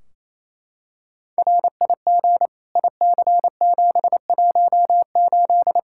Morse code, heard as a keyed tone, about 28 wpm, 700 Hz. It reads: RIG IC718
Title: Practice QSO - Exchange 2 - Info - 28wpm